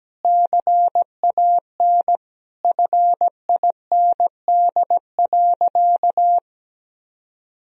Morse code, heard as a keyed tone, about 17 wpm, 700 Hz. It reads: CAN FIND.
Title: Sentences from Top 1000 Words - Review Part 2 - 17wpm